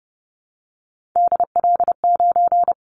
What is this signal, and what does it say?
Morse code, heard as a keyed tone, about 30 wpm, 700 Hz. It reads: DL9
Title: Callsign Prefixes for Top 25 DXCC - Repeated 3x - 30wpm